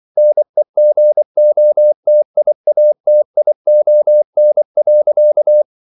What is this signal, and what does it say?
Morse code, heard as a keyed tone, about 24 wpm, 600 Hz. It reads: NEGOTIATION.